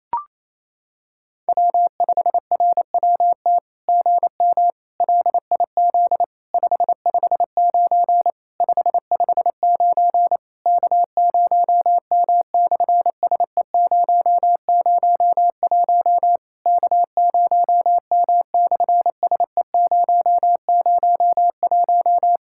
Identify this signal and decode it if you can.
Morse code, heard as a keyed tone, about 28 wpm, 700 Hz. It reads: W5RWT GM LIZ 559 559 K0M/SE001 K0M/SE001
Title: SOTA QSO Practice 28wpm